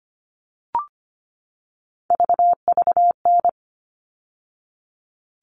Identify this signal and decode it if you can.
Morse code, heard as a keyed tone, about 25 wpm, 700 Hz. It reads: VVN
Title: All Cut Numbers - 3-Digit 25wpm